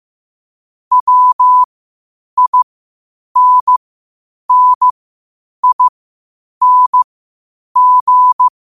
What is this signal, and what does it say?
Morse code, heard as a keyed tone, about 15 wpm, spaced as for 8 wpm, 1000 Hz. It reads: WINNING
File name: qotdfiles / 8wpm